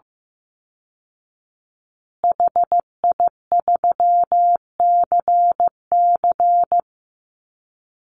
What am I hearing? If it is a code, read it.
Morse code, heard as a keyed tone, about 15 wpm, 700 Hz. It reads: HI3CC
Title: Call Signs Repeated 3x - 15wpm